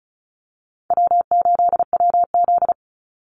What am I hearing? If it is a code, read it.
Morse code, heard as a keyed tone, about 34 wpm, 700 Hz. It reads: W8WZ
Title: Easy Call Signs Repeated 3x - 35wpm